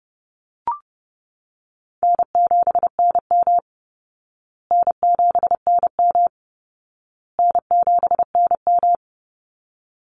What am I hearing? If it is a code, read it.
Morse code, heard as a keyed tone, about 30 wpm, 700 Hz. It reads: N7NM N7NM N7NM
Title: Easy Call Signs Repeated 3x - 30wpm